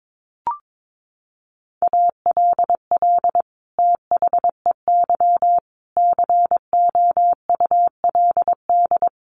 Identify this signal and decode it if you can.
Morse code, heard as a keyed tone, about 22 wpm, 700 Hz. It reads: ALL THEY COULD
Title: Sentences from Top 1300 Words 22wpm